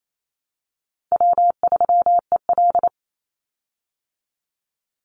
Morse code, heard as a keyed tone, about 28 wpm, 700 Hz. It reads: W3EL
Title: Easy Call Signs Repeated 3x - 28wpm